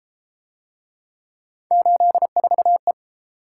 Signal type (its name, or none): Morse code